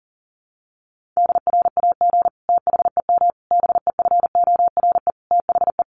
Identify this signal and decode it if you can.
Morse code, heard as a keyed tone, about 40 wpm, 700 Hz. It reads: DRAG THEM BEFORE THE
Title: Sentences from Top 1500 Words 40wpm